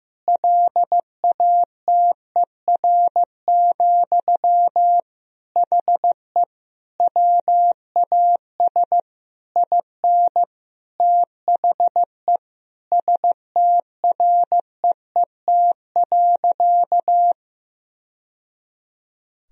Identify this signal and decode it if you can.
Morse code, heard as a keyed tone, about 16 wpm, 700 Hz. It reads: LATER, HE WAS IN THE STREET.
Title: Sentences from Top 1000 Words - Review Part 4 - 15wpm